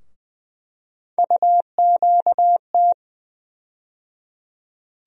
Morse code, heard as a keyed tone, about 20 wpm, 700 Hz. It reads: UQT
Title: USA Call Sign Postfixes - 3 Letters 20wpm